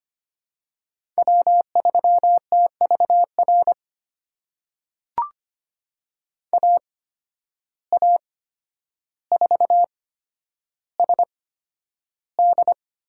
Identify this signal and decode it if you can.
Morse code, heard as a keyed tone, about 25 wpm, spaced as for 11 wpm, 700 Hz. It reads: W3TVR AA4SD
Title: USA Call Signs - 5 Characters - 8x Character Spacing - 25wpm